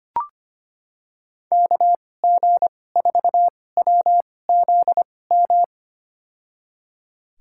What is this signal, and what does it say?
Morse code, heard as a keyed tone, about 25 wpm, spaced as for 18 wpm, 700 Hz. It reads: KG4WZM
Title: USA Call Signs - 6 Characters - 2x Character Spacing - 25wpm